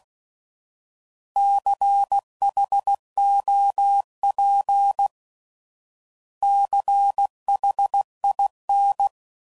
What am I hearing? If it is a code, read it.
Morse code, heard as a keyed tone, about 16 wpm, 800 Hz. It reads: CHOP CHIN
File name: Prefixes / By First Letter / Cs